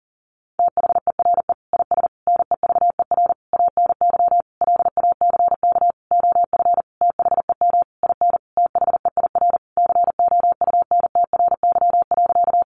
Morse code, heard as a keyed tone, about 40 wpm, 700 Hz. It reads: THERE IS NEVER ANY LACK OF THEM IN THEIR COUNTRY.
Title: Sentences from Top 1100 Words 40wpm